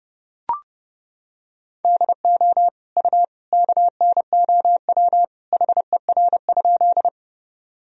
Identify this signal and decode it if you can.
Morse code, heard as a keyed tone, about 32 wpm, 700 Hz. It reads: DO U KNOW HER?
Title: Sentences from Top 100 Words and Abbreviations 30wpm